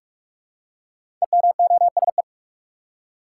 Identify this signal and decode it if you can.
Morse code, heard as a keyed tone, about 44 wpm, 700 Hz. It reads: EMOSE
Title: Code Groups - 5 Letters - 45wpm